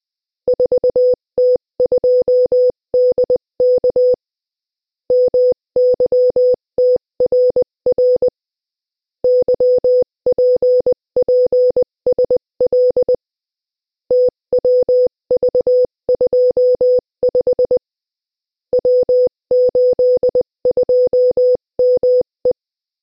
Morse code, heard as a keyed tone, about 20 wpm, 500 Hz. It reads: 4T2DK MYTRR YPPSL TWV25 W82ME
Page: Приём на слух сигналов кода Морзе - Тестовое задание 3